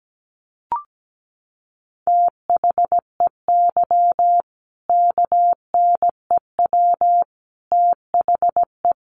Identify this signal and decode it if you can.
Morse code, heard as a keyed tone, about 17 wpm, 700 Hz. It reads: THEY KNEW THE
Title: Sentences from Top 1400 Words 17wpm